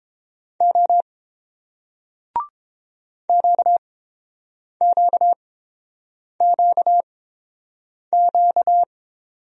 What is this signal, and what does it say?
Morse code, heard as a keyed tone, about 30 wpm, spaced as for 7 wpm, 700 Hz. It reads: O QQQQ